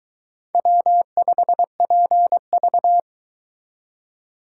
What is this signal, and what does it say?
Morse code, heard as a keyed tone, about 23 wpm, 700 Hz. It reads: W5PV